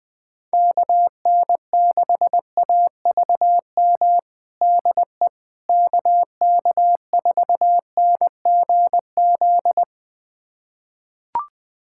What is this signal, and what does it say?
Morse code, heard as a keyed tone, about 21 wpm, 700 Hz. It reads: KN6AVM DE KK4NGZ
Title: USA Callsign DE Callsign 20wpm